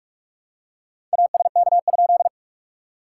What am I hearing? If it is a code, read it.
Morse code, heard as a keyed tone, about 46 wpm, 690 Hz. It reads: ASK?
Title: SOTA QSO Practice 45wpm